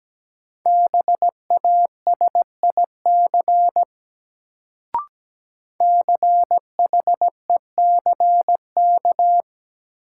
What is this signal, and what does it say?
Morse code, heard as a keyed tone, about 17 wpm, 700 Hz. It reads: BASIC CHECK
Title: Sentences from Top 1000 Words - Review Part 3 - New Word Drill - 17wpm